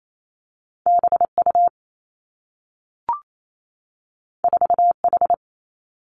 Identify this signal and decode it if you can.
Morse code, heard as a keyed tone, about 28 wpm, 700 Hz. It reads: B U 4 H
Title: Two Challenging Letter-Number Combinations 28wpm